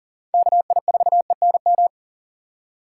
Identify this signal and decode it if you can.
Morse code, heard as a keyed tone, about 40 wpm, 700 Hz. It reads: KI4ENM
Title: USA Call Signs - 6 Characters - 3x Character Spacing - 40wpm